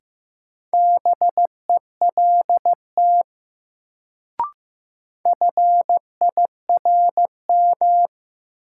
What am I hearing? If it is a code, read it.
Morse code, heard as a keyed tone, about 15 wpm, 700 Hz. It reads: BELT FIRM
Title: Sentences from Top 1400 Words - New Word Drill - 15wpm